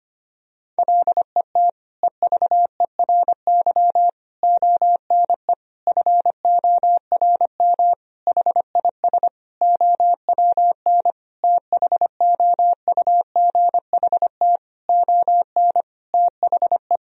Morse code, heard as a keyed tone, about 25 wpm, 700 Hz. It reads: LET EVERY ONE FORM HIS OWN THOUGHT ON THE
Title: Sentences from Top 1000 Words - Review Part 1 - 25wpm